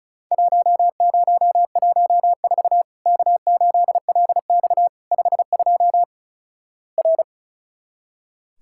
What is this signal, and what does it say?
Morse code, heard as a keyed tone, about 36 wpm, 700 Hz. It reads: 1014 K8LX 52 R
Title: Worked All Europe DX Contest - QTC Exchange - 35wpm